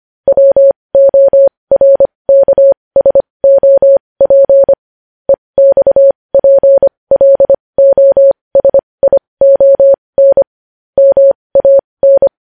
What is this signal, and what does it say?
Morse code, heard as a keyed tone, about 25 wpm, spaced as for 20 wpm, 550 Hz. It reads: WORKSOP EXPLOSION MAN
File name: podcast / News headlines-1x-25x20